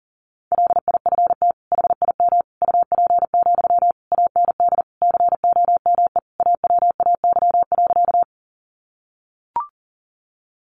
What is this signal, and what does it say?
Morse code, heard as a keyed tone, about 40 wpm, 700 Hz. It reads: LIFT HIM UP, AND COME AWAY.
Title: Sentences from Top 1000 Words 40wpm